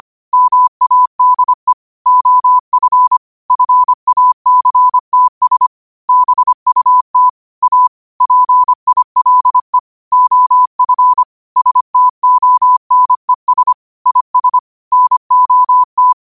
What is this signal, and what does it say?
Morse code, heard as a keyed tone, about 25 wpm, 1000 Hz. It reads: MADE OF FACTS BUT A PILE OF STONES IS NOT
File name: qotdfiles / 25wpm